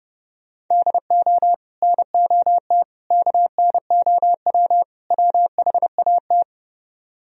Morse code, heard as a keyed tone, about 30 wpm, 700 Hz. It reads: DO NOT KNOW WHAT